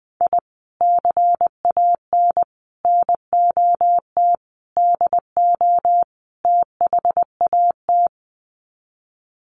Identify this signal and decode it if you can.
Morse code, heard as a keyed tone, about 20 wpm, 700 Hz. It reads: I CAN NOT DO THAT
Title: Sets of 5 Words Encore2 20wpm